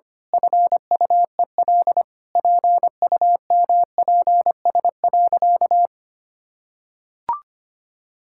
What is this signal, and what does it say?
Morse code, heard as a keyed tone, about 25 wpm, 700 Hz. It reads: FUEL PUMPS.
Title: Pangram Challenge v2 - 25wpm